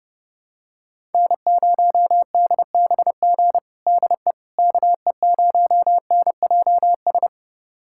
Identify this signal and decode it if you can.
Morse code, heard as a keyed tone, about 30 wpm, 700 Hz. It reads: N0DBG DE KE0NJS
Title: USA Callsign DE Callsign 30wpm